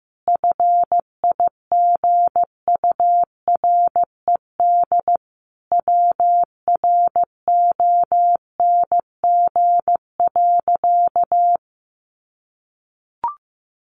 Morse code, heard as a keyed tone, about 15 wpm, 700 Hz. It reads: FIGURED WRONG.
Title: A Sound Of Thunder 15wpm